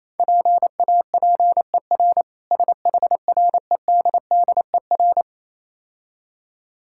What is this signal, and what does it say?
Morse code, heard as a keyed tone, about 28 wpm, 700 Hz. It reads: PAPER SHREDDER